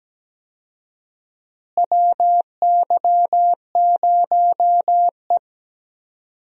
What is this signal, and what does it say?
Morse code, heard as a keyed tone, about 17 wpm, 700 Hz. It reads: WY0E
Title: USA Call Signs - 4 Characters - Repeated 3x - 17wpm